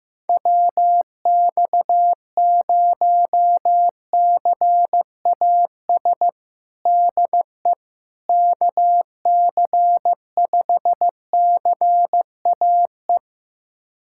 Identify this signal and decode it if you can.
Morse code, heard as a keyed tone, about 15 wpm, 700 Hz. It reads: WX0CAS DE KC5CAE
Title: USA Callsign DE Callsign - 6 Characters - 15wpm